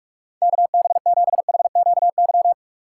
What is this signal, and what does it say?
Morse code, heard as a keyed tone, about 46 wpm, 700 Hz. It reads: KB7HQY
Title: USA Call Signs - 6 Characters - 2x Character Spacing - 45wpm